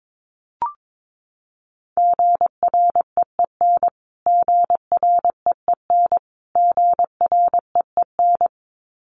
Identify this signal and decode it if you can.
Morse code, heard as a keyed tone, about 22 wpm, 700 Hz. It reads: GREEN GREEN GREEN
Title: Top 500 Words Repeated 22wpm